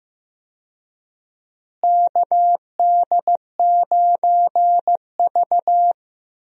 Morse code, heard as a keyed tone, about 15 wpm, 700 Hz. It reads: KD9V
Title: USA Call Signs - 4 Characters - Repeated 2x - 15wpm